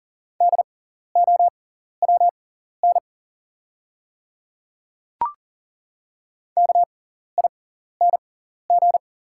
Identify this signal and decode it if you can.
Morse code, heard as a keyed tone, about 40 wpm, spaced as for 14 wpm, 690 Hz. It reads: DOWN KING
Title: 6x Character Spacing - Top 500 Words - 40wpm